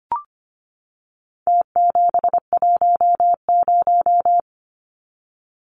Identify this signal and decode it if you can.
Morse code, heard as a keyed tone, about 25 wpm, 700 Hz. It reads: T710
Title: Numbers - Contest Style - 4-Digit 25wpm